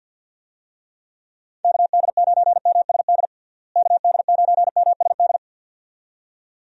Morse code, heard as a keyed tone, about 50 wpm, 700 Hz. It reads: KD9MSD KD9MSD
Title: USA Call Signs - 6 Characters - Repeated 2x - 50wpm